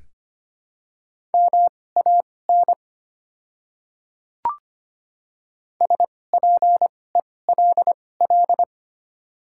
Morse code, heard as a keyed tone, about 25 wpm, spaced as for 18 wpm, 700 Hz. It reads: MAN SPELL
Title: 2x Character Spacing - Top 300 Words - 25wpm